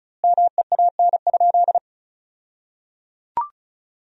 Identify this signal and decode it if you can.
Morse code, heard as a keyed tone, about 36 wpm, 700 Hz. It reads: MEAN?
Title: 2x Word-Spacing - Sentences from Top 200 Words - 35wpm